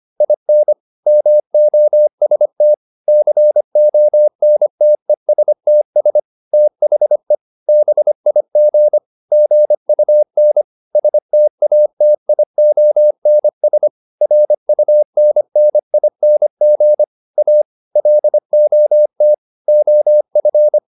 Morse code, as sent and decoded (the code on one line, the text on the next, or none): .. -. / -- --- ... - / -.-. --- -. - . ... - ... / - .... / -... .. --. / --. ..- -. / ... - .- - .. --- -. ... / .-. ..- -. -. .. -. --. / .- / .-.. --- - / --- ..-.
IN MOST CONTESTS THE BIG GUN STATIONS RUNNING A LOT OF